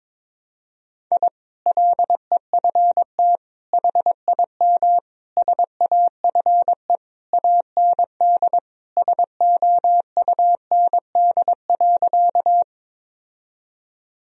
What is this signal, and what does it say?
Morse code, heard as a keyed tone, about 22 wpm, 700 Hz. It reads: I LEFT HIM SAFE AND SOUND.
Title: Sentences from Top 1000 Words - Review Part 1 - 22wpm